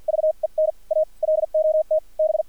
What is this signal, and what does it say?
Morse code, heard as a keyed tone, about 50 wpm, spaced as for 42 wpm, 630 Hz, with radio noise. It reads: VEN A LOT B
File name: archived / CRSnet